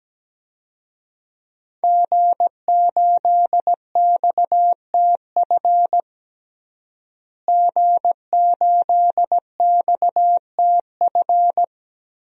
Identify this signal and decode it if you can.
Morse code, heard as a keyed tone, about 17 wpm, 700 Hz. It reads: G8XTF G8XTF
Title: DXCC-100 Call Signs - 5 Characters - Repeated 3x - 17wpm